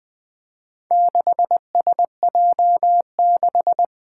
Morse code, heard as a keyed tone, about 20 wpm, 700 Hz. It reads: 6SJ6
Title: Four Challenging Letter-Number Combinations 20wpm